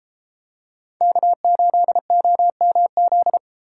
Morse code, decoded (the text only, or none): K8OMZ